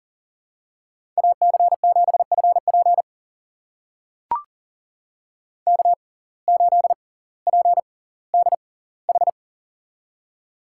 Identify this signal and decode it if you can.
Morse code, heard as a keyed tone, about 40 wpm, spaced as for 20 wpm, 700 Hz. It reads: AC7FP K8PDH
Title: USA Call Signs - 5 Characters - 6x Character Spacing - 40wpm